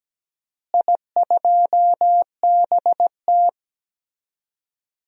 Morse code, heard as a keyed tone, about 17 wpm, 700 Hz. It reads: I2BT
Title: DXCC-100 Call Signs - 4 Characters - Repeated 3x - 17wpm